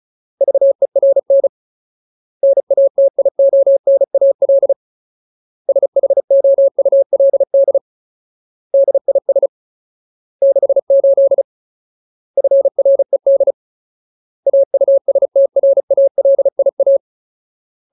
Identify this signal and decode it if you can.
Morse code, heard as a keyed tone, about 34 wpm, 550 Hz. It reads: VERN NATIONAL SHOULD DIS 68 FRED AUSTRALIA
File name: Ultimate Word List / 35WPM Ultimate